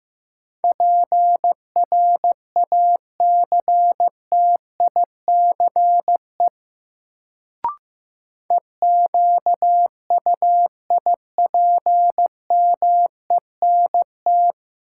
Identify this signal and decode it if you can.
Morse code, heard as a keyed tone, about 15 wpm, 700 Hz. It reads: PRACTICE EQUIPMENT
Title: Sentences from Top 700 Words - New Word Drill - 15wpm